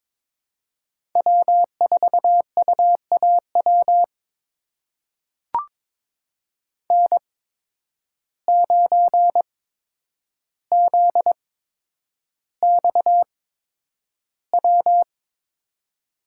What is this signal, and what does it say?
Morse code, heard as a keyed tone, about 22 wpm, spaced as for 9 wpm, 700 Hz. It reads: W4UAW N9ZXW
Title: USA Call Signs - 5 Characters - 8x Character Spacing - 22wpm